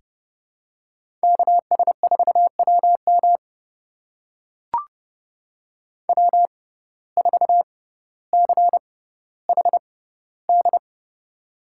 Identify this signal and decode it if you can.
Morse code, heard as a keyed tone, about 29 wpm, spaced as for 15 wpm, 700 Hz. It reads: KS4WM W4CHD
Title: USA Call Signs - 5 Characters - 6x Character Spacing - 30wpm